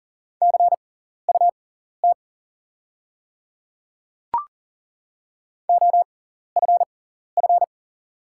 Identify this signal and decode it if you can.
Morse code, heard as a keyed tone, about 41 wpm, spaced as for 14 wpm, 690 Hz. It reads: CUT OFF
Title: 6x Character Spacing - Top 300 Words - 40wpm